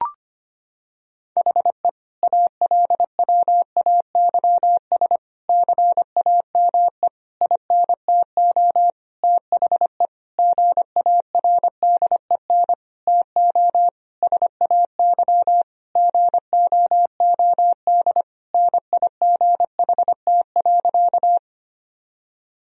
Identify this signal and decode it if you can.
Morse code, heard as a keyed tone, about 25 wpm, 700 Hz. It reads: HE ALWAYS CAME INTO THE GARDEN TO SAY GOOD NIGHT.
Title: Sentences from Top 1000 Words - Review Part 4 - 25wpm